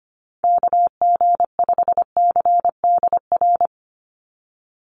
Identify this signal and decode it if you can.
Morse code, heard as a keyed tone, about 25 wpm, 700 Hz. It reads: KG5CDR
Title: USA Call Signs 25wpm